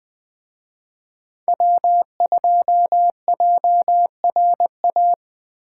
Morse code, heard as a keyed tone, about 21 wpm, 700 Hz. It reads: W2JRA
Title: Speed Racing - USA 1x3 Call Signs 17wpm